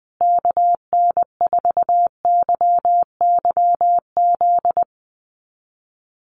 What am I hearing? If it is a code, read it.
Morse code, heard as a keyed tone, about 20 wpm, 700 Hz. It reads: KN4YYZ
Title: USA Call Signs Repeated 3x 20wpm